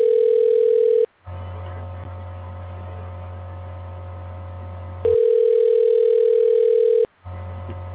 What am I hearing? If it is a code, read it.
Ringback tone (North America), 440 + 480 Hz.
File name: recordings / RE1f459ff8fef0dce8dda79701b8c9d370